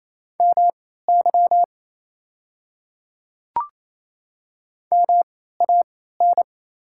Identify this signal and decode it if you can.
Morse code, heard as a keyed tone, about 28 wpm, spaced as for 16 wpm, 700 Hz. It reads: MY MAN
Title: 3x Character Spacing - Top 500 Words - 28wpm